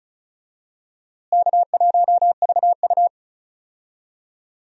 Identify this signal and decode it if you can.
Morse code, heard as a keyed tone, about 34 wpm, 700 Hz. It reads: K1VU
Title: USA Call Signs - 4 Characters - Repeated 3x - 35wpm